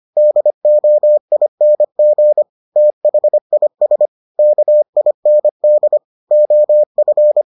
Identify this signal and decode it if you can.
Morse code, heard as a keyed tone, about 25 wpm, 600 Hz. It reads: DOING THIS KIND OF